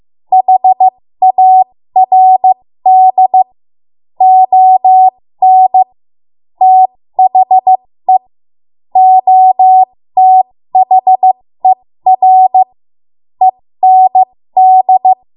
Morse code, heard as a keyed tone, about 15 wpm, spaced as for 13 wpm, 750 Hz. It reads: HARD ON THE OTHER END
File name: Morse / 13WPM